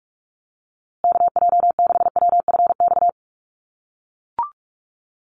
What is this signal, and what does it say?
Morse code, heard as a keyed tone, about 44 wpm, 700 Hz. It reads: KJ6WFX
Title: USA Call Signs Repeated 3x 45wpm